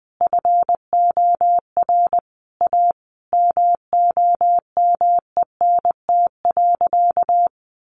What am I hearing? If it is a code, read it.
Morse code, heard as a keyed tone, about 20 wpm, 700 Hz. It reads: FOR A MOMENT.
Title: Sentences from Top 1500 Words 20wpm